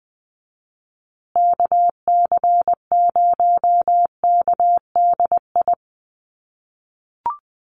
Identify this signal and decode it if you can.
Morse code, heard as a keyed tone, about 20 wpm, 700 Hz. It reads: KC0KDI
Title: USA Call Signs Repeated 3x 20wpm